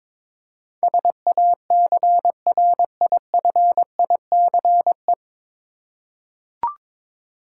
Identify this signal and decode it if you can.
Morse code, heard as a keyed tone, about 22 wpm, 700 Hz. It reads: SACRIFICE
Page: Morse Code Ninja practice 22wpm